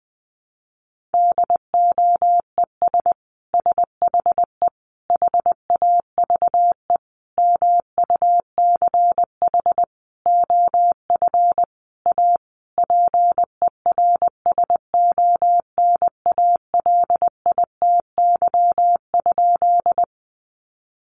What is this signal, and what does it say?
Morse code, heard as a keyed tone, about 20 wpm, 700 Hz. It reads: DOES SHE HAVE MUCH OF A PERSONALITY?
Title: Sentences from Top 1400 Words 20wpm